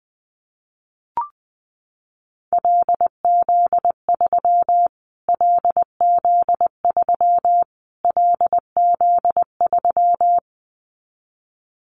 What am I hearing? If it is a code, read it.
Morse code, heard as a keyed tone, about 20 wpm, 700 Hz. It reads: LZ3 LZ3 LZ3
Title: Callsign Prefixes for Top 25 DXCC - Repeated 3x - 20wpm